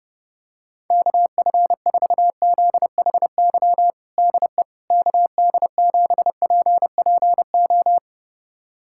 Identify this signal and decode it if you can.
Morse code, heard as a keyed tone, about 30 wpm, 700 Hz. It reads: KF4ZHY DE KD7PPO
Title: USA Callsign DE Callsign 30wpm